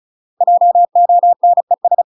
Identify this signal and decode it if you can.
Morse code, heard as a keyed tone, about 36 wpm, 700 Hz. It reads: JONES
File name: UK MASTER QSOs TRAINING / 35WPM / UK MASTER QSOs 0013